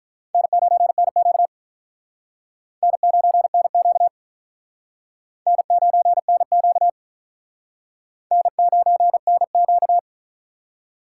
Morse code, heard as a keyed tone, about 44 wpm, 700 Hz. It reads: N9NQ N9NQ N9NQ N9NQ
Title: Speed Racing - USA 1x2 Call Signs 35wpm